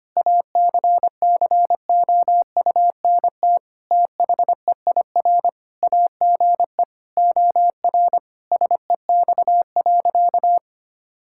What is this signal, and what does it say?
Morse code, heard as a keyed tone, about 25 wpm, 700 Hz. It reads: ACCOUNT THEIR AGE OR SEX.
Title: Sentences from Top 1600 Words 25wpm